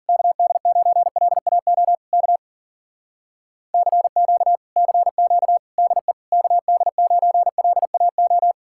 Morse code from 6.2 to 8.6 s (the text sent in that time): KD9LAO